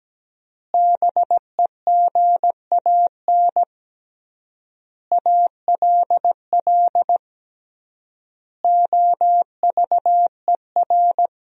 Morse code, heard as a keyed tone, about 17 wpm, 700 Hz. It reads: BEGAN ALL OVER
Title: 3x Word-Spacing - Sentences from Top 500 Words - 17wpm